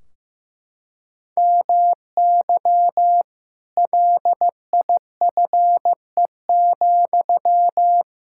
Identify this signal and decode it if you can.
Morse code, heard as a keyed tone, about 15 wpm, 700 Hz. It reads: MY LIFE,
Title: Sentences from Top 1600 Words 15wpm